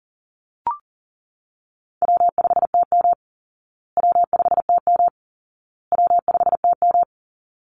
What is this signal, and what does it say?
Morse code, heard as a keyed tone, about 37 wpm, 700 Hz. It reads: W5TM W5TM W5TM
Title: Easy Call Signs Repeated 3x - 40wpm